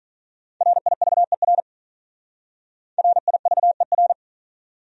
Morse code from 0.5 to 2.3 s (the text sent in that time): RIVER